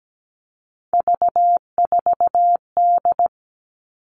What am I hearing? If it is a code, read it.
Morse code, heard as a keyed tone, about 17 wpm, 700 Hz. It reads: V4D
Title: All Cut Numbers - 3-Digit 17wpm